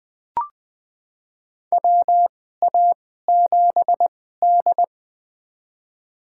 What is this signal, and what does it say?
Morse code, heard as a keyed tone, about 20 wpm, spaced as for 15 wpm, 700 Hz. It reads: WA7D
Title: USA Call Signs - 4 Characters - 2x Character Spacing - 20wpm